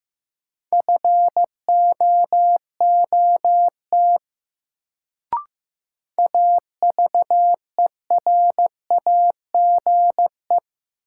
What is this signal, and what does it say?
Morse code, heard as a keyed tone, about 16 wpm, 700 Hz. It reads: FOOT AVERAGE
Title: Sets of 5 Words - New Word Drill - 15wpm